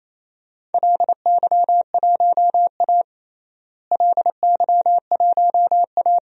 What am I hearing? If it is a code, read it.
Morse code, heard as a keyed tone, about 28 wpm, 700 Hz. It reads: LY1A LY1A
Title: DXCC-100 Call Signs - 4 Characters - Repeated 3x - 28wpm